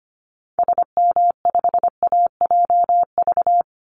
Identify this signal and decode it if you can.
Morse code, heard as a keyed tone, about 25 wpm, 700 Hz. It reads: SM5AJV
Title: Call Signs Repeated 3x - 25wpm